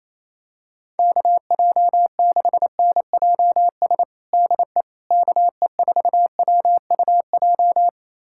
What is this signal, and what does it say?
Morse code, heard as a keyed tone, about 27 wpm, 700 Hz. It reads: KJ6NJS DE KE4WUJ
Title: USA Callsign DE Callsign 28wpm